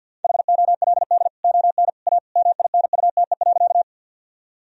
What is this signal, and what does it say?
Morse code, heard as a keyed tone, about 50 wpm, 700 Hz. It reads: HOLD ON A MINUTE.
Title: Sentences from Top 1000 Words - Review Part 2 - 50wpm